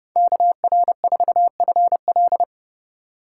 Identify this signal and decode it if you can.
Morse code, heard as a keyed tone, about 28 wpm, 700 Hz. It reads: KR4FL